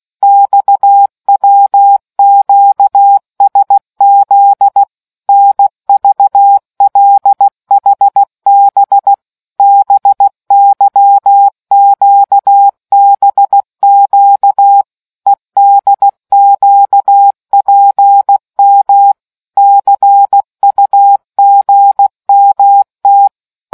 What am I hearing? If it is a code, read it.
Morse code, heard as a keyed tone, about 16 wpm, 800 Hz. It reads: XWQSZ NVLHB BYQBQ EDQPM CUGMT